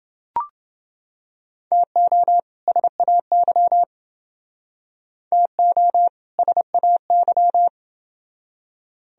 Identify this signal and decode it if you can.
Morse code, heard as a keyed tone, about 29 wpm, 700 Hz. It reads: TO SAY TO SAY